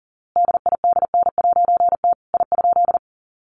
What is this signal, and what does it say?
Morse code, heard as a keyed tone, about 40 wpm, 700 Hz. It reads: DIDN'T I?
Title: A Sound Of Thunder 40wpm